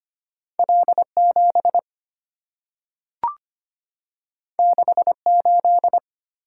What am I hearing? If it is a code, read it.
Morse code, heard as a keyed tone, about 25 wpm, 700 Hz. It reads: L7 68
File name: Two Letter-Number Combinations - 25wpm